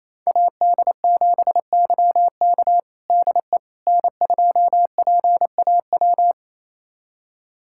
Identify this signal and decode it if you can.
Morse code, heard as a keyed tone, about 28 wpm, 700 Hz. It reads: AD7YK DE N2PAW
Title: USA Callsign DE Callsign - 5 Characters - 28wpm